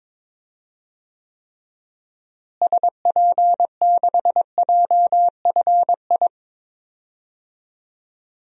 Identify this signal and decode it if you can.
Morse code, heard as a keyed tone, about 21 wpm, 700 Hz. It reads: SP6JFI
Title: DXCC-100 Call Signs - 6 Characters - Repeated 2x - 22wpm